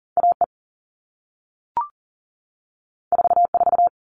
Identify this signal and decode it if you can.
Morse code, heard as a keyed tone, about 40 wpm, 700 Hz. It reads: AE 44
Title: All Cut Numbers - 2-Digit 40wpm